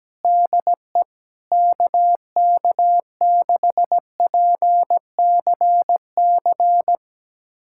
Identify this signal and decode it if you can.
Morse code, heard as a keyed tone, about 17 wpm, 700 Hz. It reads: DE KK6PCC
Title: USA Callsign DE Callsign 17wpm